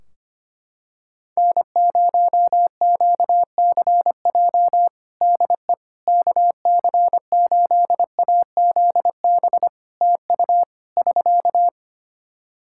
Morse code, heard as a keyed tone, about 25 wpm, 700 Hz. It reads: N0QCJ DE KC8AZB TU <SK>
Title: Practice QSO - Exchange 4 - Ending - 25wpm